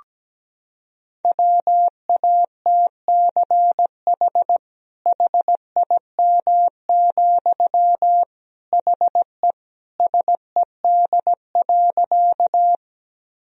Morse code, heard as a keyed tone, about 17 wpm, 700 Hz. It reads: WATCH HIM, HE SED.
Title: Sentences from Top 300 Words and Abbreviations 17wpm